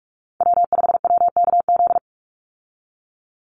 Morse code, heard as a keyed tone, about 44 wpm, 700 Hz. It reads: W5WKZ
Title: USA Call Signs 45wpm